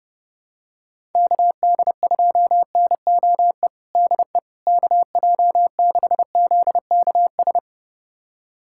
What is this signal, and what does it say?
Morse code, heard as a keyed tone, about 30 wpm, 700 Hz. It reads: KD2NOE DE KJ6ZKS
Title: USA Callsign DE Callsign - 6 Characters - 30wpm